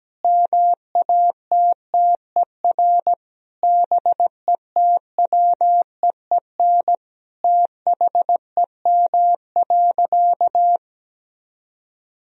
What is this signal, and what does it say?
Morse code, heard as a keyed tone, about 17 wpm, 700 Hz. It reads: MATTER BETWEEN THEM.
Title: Sentences from Top 1000 Words - Review Part 4 - 17wpm